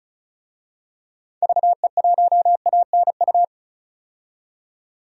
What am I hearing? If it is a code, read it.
Morse code, heard as a keyed tone, about 36 wpm, 700 Hz. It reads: VE1ANU
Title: DXCC-100 Call Signs - 6 Characters - 35wpm